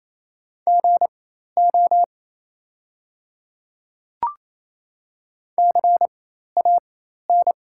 Morse code, heard as a keyed tone, about 28 wpm, spaced as for 13 wpm, 700 Hz. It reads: GO CAN
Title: 4x Character Spacing - Top 100 Words - 28wpm